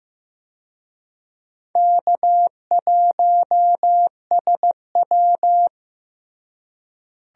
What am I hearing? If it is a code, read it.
Morse code, heard as a keyed tone, about 15 wpm, 700 Hz. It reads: K1SW
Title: USA Call Signs - 4 Characters - Repeated 3x - 15wpm